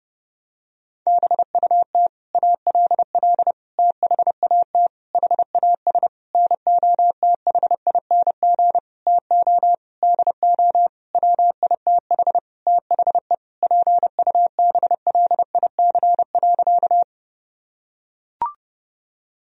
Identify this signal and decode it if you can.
Morse code, heard as a keyed tone, about 32 wpm, 700 Hz. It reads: BUT ALL THAT HAS NOTHING TO DO WITH THE PUBLIC.
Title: Sentences from Top 1000 Words - Review Part 3 - 30wpm